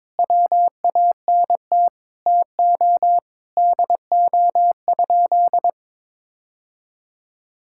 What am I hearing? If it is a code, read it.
Morse code, heard as a keyed tone, about 22 wpm, 700 Hz. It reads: WANT TO DO?
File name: practice / 22wpm / Sentences with 3-Word Phrases - 22wpm